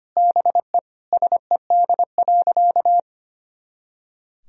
Morse code, heard as a keyed tone, about 25 wpm, 700 Hz. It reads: BE SED.
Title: Sentences from Top 100 Words and Abbreviations 25wpm